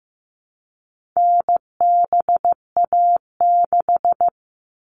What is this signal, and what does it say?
Morse code, heard as a keyed tone, about 15 wpm, 700 Hz. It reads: NBA6
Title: All Cut Numbers - 4-Digit 15wpm